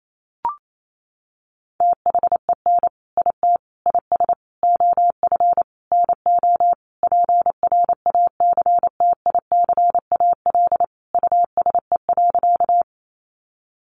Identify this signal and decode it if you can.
Morse code, heard as a keyed tone, about 28 wpm, 700 Hz. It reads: THEN IT IS OF NO PRACTICAL USE.
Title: Sentences from Top 1400 Words 28wpm